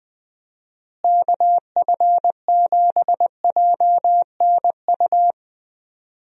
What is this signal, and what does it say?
Morse code, heard as a keyed tone, about 20 wpm, 700 Hz. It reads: KF7JNU
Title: USA Call Signs - 6 Characters - Repeated 2x - 20wpm